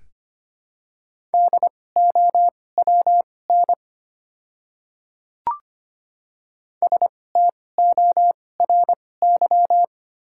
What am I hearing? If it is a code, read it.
Morse code, heard as a keyed tone, about 25 wpm, spaced as for 18 wpm, 700 Hz. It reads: DOWN STORY
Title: 2x Character Spacing - Top 300 Words - 25wpm